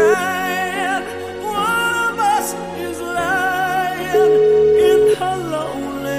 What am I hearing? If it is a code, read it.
Ringback tone (Europe, 425 Hz), with noise.